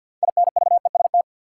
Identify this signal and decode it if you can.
Morse code, heard as a keyed tone, about 50 wpm, 700 Hz. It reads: INVEST
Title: Sentences from Top 1100 Words - New Word Drill - 50wpm